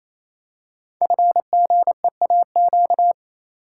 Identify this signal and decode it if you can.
Morse code, heard as a keyed tone, about 27 wpm, 700 Hz. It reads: FGEAQ
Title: Code Groups - 5 Letters - 28wpm